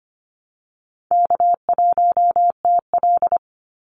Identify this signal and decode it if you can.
Morse code, heard as a keyed tone, about 25 wpm, 700 Hz. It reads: K1TL
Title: Easy Call Signs - 25wpm